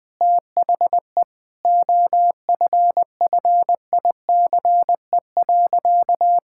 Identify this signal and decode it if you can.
Morse code, heard as a keyed tone, about 20 wpm, 700 Hz. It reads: THE OFFICE.
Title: Sentences with 3-Word Phrases - Spoken Beginning - 20wpm